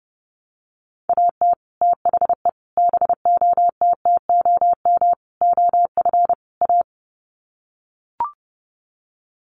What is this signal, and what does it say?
Morse code, heard as a keyed tone, about 30 wpm, 700 Hz. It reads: AT THE BOTTOM OF A
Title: Sets of 5 Words Encore2 30wpm